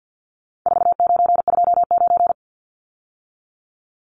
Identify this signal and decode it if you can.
Morse code, heard as a keyed tone, about 50 wpm, 700 Hz. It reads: 4929
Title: Numbers - 4-Digit 50wpm